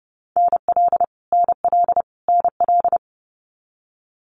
Morse code, heard as a keyed tone, about 30 wpm, 700 Hz. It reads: NL NL NL
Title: Canadian Provinces and Territories - Repeated 3x - 30wpm